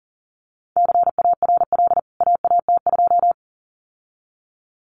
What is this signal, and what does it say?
Morse code, heard as a keyed tone, about 40 wpm, 700 Hz. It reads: CARL AAT2
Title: CWT Exchange 40wpm